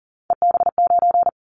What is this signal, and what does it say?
Morse code, heard as a keyed tone, about 40 wpm, 700 Hz. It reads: EB9
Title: Callsign Prefixes for Top 100 DXCC - 40wpm